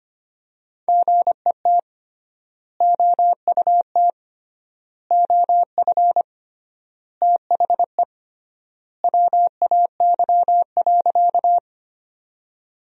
Morse code, heard as a keyed tone, about 25 wpm, 700 Hz. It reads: GET OUT OF THE WAY.
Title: 3x Word-Spacing - Sentences from Top 200 Words - 25wpm